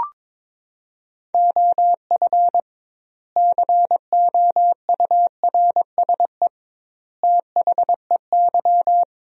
Morse code, heard as a keyed tone, about 22 wpm, 700 Hz. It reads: OF COURSE THEY
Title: 2x Word-Spacing - Sentences from Top 500 Words - 22wpm